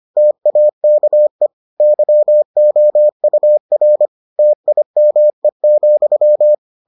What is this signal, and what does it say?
Morse code, heard as a keyed tone, about 25 wpm, 600 Hz. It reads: TAKE YOUR TIME,